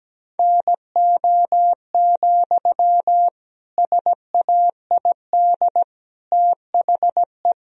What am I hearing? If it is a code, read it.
Morse code, heard as a keyed tone, about 17 wpm, 700 Hz. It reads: NO, SAID THE
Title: Sentences from Top 1000 Words - Review Part 4 - 17wpm